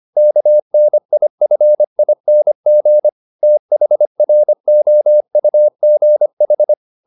Morse code, heard as a keyed tone, about 25 wpm, 600 Hz. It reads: KNIFING THROUGH